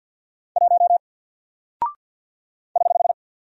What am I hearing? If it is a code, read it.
Morse code, heard as a keyed tone, about 50 wpm, 700 Hz. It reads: 1 <HH>
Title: Single Character (Letter:Number:Punctuation:Prosign) Alt 50wpm